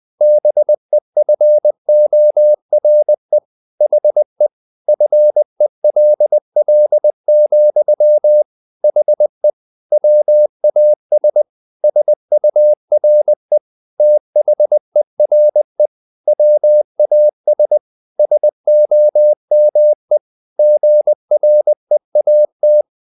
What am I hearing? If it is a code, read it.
Morse code, heard as a keyed tone, about 20 wpm, 600 Hz. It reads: BEFORE HE FELL, HE WAS SURE THERE WAS SOME GREAT
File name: the old man and the sea part 2 20wpm